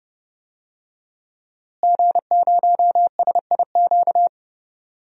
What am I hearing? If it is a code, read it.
Morse code, heard as a keyed tone, about 31 wpm, 700 Hz. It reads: G0SIQ
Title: DXCC-100 Call Signs - 5 Characters - Repeated 2x - 30wpm